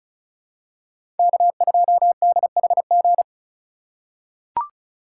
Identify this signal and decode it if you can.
Morse code, heard as a keyed tone, about 36 wpm, 700 Hz. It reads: K2DHG
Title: USA Call Signs - 5 Characters - 6x Character Spacing - 35wpm